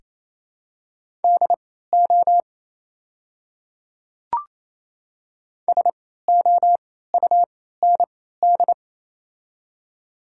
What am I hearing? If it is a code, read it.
Morse code, heard as a keyed tone, about 28 wpm, spaced as for 16 wpm, 700 Hz. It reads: DO SOUND
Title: 3x Character Spacing - Top 100 Words - 28wpm